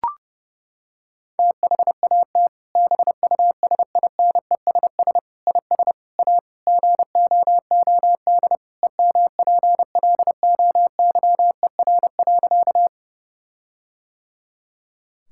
Morse code, heard as a keyed tone, about 30 wpm, 700 Hz. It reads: THAT BUSINESS IS A GOOD EMPLOYER.
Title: Sentences from Top 1600 Words 30wpm